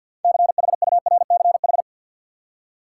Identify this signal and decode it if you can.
Morse code, heard as a keyed tone, about 53 wpm, 700 Hz. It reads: CHURCH
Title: Six Letter Words - 50wpm